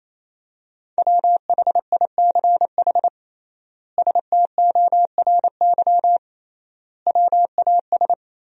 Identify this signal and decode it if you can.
Morse code, heard as a keyed tone, about 28 wpm, 700 Hz. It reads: WHICH STORY WAS
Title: 3x Word-Spacing - Sentences from Top 300 Words - 28wpm